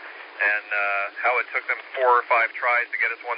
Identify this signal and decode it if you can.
Emergency Alert System (SAME) tones, heard as harsh buzzing bursts, with speech. It is heard from 0.3 to 3.2 s.